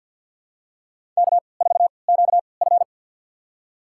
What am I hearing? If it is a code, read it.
Morse code, heard as a keyed tone, about 52 wpm, spaced as for 29 wpm, 700 Hz. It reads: K4QF
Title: USA Call Signs - 4 Characters - 3x Character Spacing - 50wpm